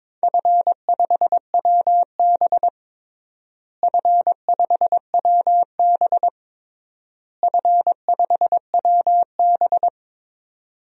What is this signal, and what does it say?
Morse code, heard as a keyed tone, about 22 wpm, 700 Hz. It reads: F5WB F5WB F5WB
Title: DXCC-100 Call Signs - 4 Characters - Repeated 3x - 22wpm